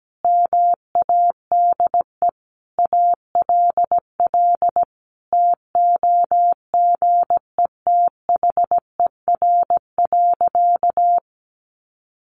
Morse code, heard as a keyed tone, about 17 wpm, 700 Hz. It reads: MADE ALL TOGETHER.
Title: Sentences from Top 1100 Words 17wpm